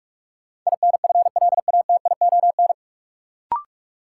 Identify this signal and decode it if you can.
Morse code, heard as a keyed tone, about 42 wpm, 700 Hz. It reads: INFLATION
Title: Sentences from Top 1100 Words - New Word Drill Alt - 45wpm